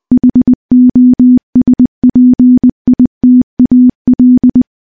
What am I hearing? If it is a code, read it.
Morse code, heard as a keyed tone, about 21 wpm, 250 Hz. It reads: HOSPITAL
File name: morse / morse-training-20WPM